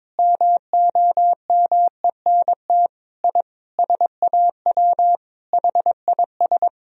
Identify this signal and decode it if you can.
Morse code, heard as a keyed tone, about 22 wpm, 700 Hz. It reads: MOMENT I SAW HIS
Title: Sentences from Top 1000 Words - Review Part 1 - 22wpm